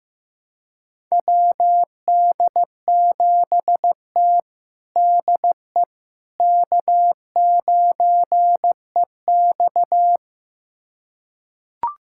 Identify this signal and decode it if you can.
Morse code, heard as a keyed tone, about 15 wpm, 700 Hz. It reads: WD7T DE K9EX
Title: USA Callsign DE Callsign - 4 Characters - 15wpm